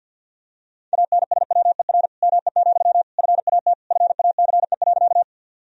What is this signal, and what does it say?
Morse code, heard as a keyed tone, about 50 wpm, 700 Hz. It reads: ANSWER ME, FAT FACE.
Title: Sentences from Top 1000 Words - Review Part 4 - 50wpm